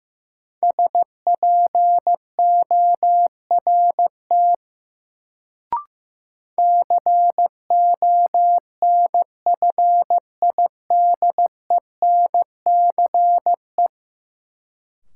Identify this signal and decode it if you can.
Morse code, heard as a keyed tone, about 15 wpm, 700 Hz. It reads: SPORT CONFIDENCE
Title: Sentences from Top 1500 Words - New Word Drill Alt - 15wpm